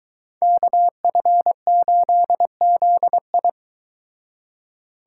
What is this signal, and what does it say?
Morse code, heard as a keyed tone, about 23 wpm, 700 Hz. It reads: KF8ZI